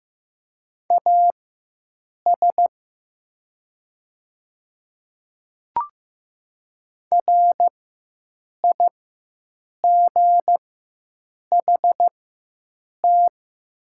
Morse code, heard as a keyed tone, about 15 wpm, spaced as for 7 wpm, 700 Hz. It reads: AS RIGHT
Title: 4x Character Spacing - Top 200 Words - 15wpm